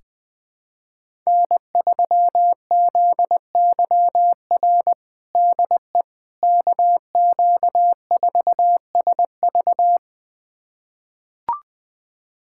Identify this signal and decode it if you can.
Morse code, heard as a keyed tone, about 19 wpm, 700 Hz. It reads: N3ZYR DE KQ4SV
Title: USA Callsign DE Callsign - 5 Characters - 20wpm